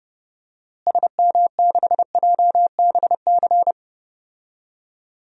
Morse code, heard as a keyed tone, about 30 wpm, 700 Hz. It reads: SM6JBC
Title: DXCC-100 Call Signs - 6 Characters - Repeated 2x - 30wpm